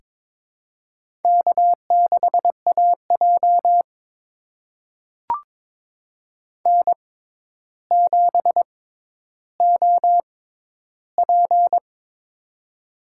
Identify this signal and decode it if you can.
Morse code, heard as a keyed tone, about 22 wpm, spaced as for 11 wpm, 700 Hz. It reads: K6AJ N7OP
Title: USA Call Signs - 4 Characters - 6x Character Spacing - 22wpm